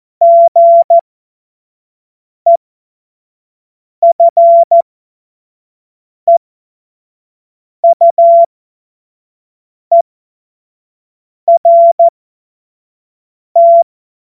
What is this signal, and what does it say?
Morse code, heard as a keyed tone, about 14 wpm, spaced as for 5 wpm, 680 Hz. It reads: GEFEUERT
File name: Morsekurs / Witze / Witz 15. November 2022 25 F70